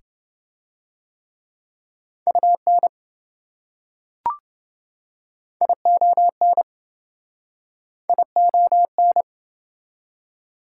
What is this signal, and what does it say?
Morse code, heard as a keyed tone, about 27 wpm, 700 Hz. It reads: UN ION ION